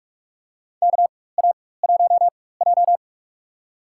Morse code, heard as a keyed tone, about 46 wpm, spaced as for 21 wpm, 700 Hz. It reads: KA1J
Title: USA Call Signs - 4x Character Spacing - 45wpm